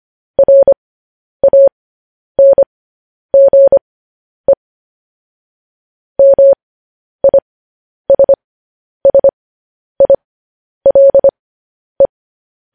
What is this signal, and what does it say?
Morse code, heard as a keyed tone, about 25 wpm, spaced as for 10 wpm, 550 Hz. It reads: RANGE MISSILE